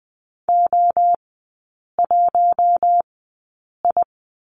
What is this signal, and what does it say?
Morse code, heard as a keyed tone, about 20 wpm, spaced as for 8 wpm, 700 Hz. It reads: O1I
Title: 067 Lesson 12e - Letter d - Three Characters 20wpm